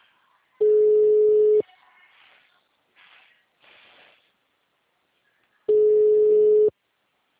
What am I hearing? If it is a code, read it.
Ringback tone (Europe, 425 Hz).